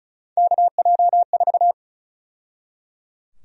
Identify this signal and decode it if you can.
Morse code, heard as a keyed tone, about 34 wpm, 700 Hz. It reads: KJ4
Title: USA Call Sign Prefixes - 35wpm